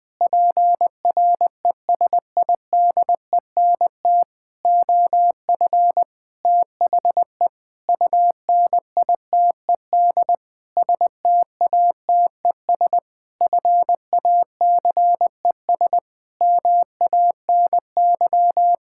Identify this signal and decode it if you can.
Morse code, heard as a keyed tone, about 20 wpm, 700 Hz. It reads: PRESIDENT OF THE UNITED STATES FACES MANY
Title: Sentences with 7-Word Phrases - Spoken Beginning - 20wpm